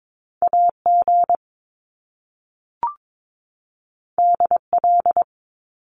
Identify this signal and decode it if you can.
Morse code, heard as a keyed tone, about 22 wpm, 700 Hz. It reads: AG DL
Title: Top Two Letter Combinations in Words 22wpm